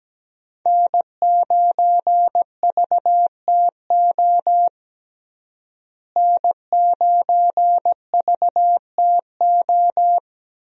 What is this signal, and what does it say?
Morse code, heard as a keyed tone, about 17 wpm, 700 Hz. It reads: N9VTO N9VTO
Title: USA Call Signs - 5 Characters - Repeated 3x - 17wpm